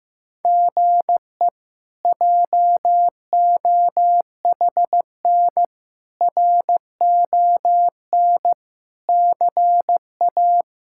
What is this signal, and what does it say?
Morse code, heard as a keyed tone, about 15 wpm, 700 Hz. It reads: GE JOHN RON CA
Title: K1USN SST QSO Practice - 15wpm